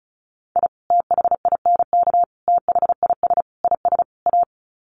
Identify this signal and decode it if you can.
Morse code, heard as a keyed tone, about 34 wpm, 700 Hz. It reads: I THINK THIS IS A
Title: Sets of 5 Words 35wpm